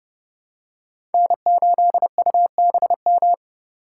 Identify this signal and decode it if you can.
Morse code, heard as a keyed tone, about 28 wpm, 700 Hz. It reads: N8UBM